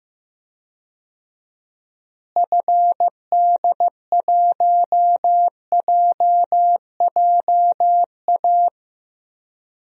Morse code, heard as a keyed tone, about 16 wpm, 700 Hz. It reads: FD1JJA
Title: DXCC-100 Call Signs - 6 Characters - Repeated 3x - 15wpm